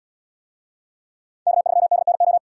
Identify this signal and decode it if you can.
Morse code, heard as a keyed tone, about 71 wpm, 700 Hz. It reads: K3DNO